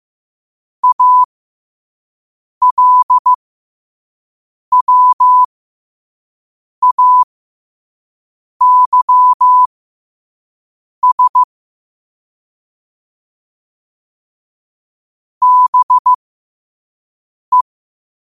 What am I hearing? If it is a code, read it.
Morse code, heard as a keyed tone, about 15 wpm, spaced as for 5 wpm, 1000 Hz. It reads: ALWAYS BE
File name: qotdfiles / 5wpm